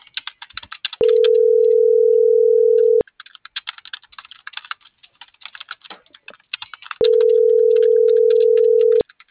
Ringback tone (North America), 440 + 480 Hz.
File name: CENTRAL MULTIMODAL / out angel.velazquez - .42071